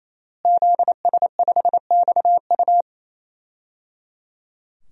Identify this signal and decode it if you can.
Morse code, heard as a keyed tone, about 27 wpm, 700 Hz. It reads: ZS5XU
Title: DXCC-100 Call Signs - 5 Characters - Repeated 3x - 28wpm